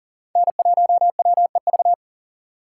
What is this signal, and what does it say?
Morse code, heard as a keyed tone, about 40 wpm, 700 Hz. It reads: N1WEV
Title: USA Call Signs - 5 Characters - 6x Character Spacing - 40wpm